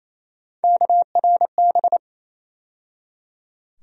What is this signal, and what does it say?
Morse code, heard as a keyed tone, about 28 wpm, 700 Hz. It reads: KRB
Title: USA Call Sign Postfixes - 3 Letters 28wpm